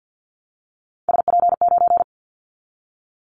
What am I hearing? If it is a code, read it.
Morse code, heard as a keyed tone, about 50 wpm, 700 Hz. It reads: SP9
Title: Callsign Prefixes for Top 25 DXCC - 50wpm